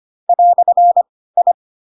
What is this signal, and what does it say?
Morse code, heard as a keyed tone, about 25 wpm, spaced as for 16 wpm, 700 Hz. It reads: "I
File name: Pinocchio / Pinocchio ch4 10 25 ik1hge